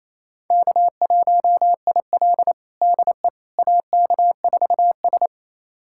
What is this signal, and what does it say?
Morse code, heard as a keyed tone, about 28 wpm, 700 Hz. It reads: K1IL DE AK4S
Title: USA Callsign DE Callsign - 4 Characters - 28wpm